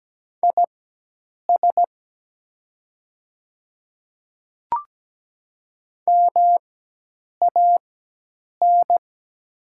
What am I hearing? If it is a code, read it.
Morse code, heard as a keyed tone, about 17 wpm, spaced as for 8 wpm, 700 Hz. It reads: IS MAN
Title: 4x Character Spacing - Top 200 Words - 17wpm